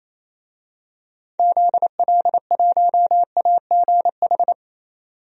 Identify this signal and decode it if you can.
Morse code, heard as a keyed tone, about 27 wpm, 700 Hz. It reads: ZL1AGH
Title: DXCC-100 Call Signs - 6 Characters - 28wpm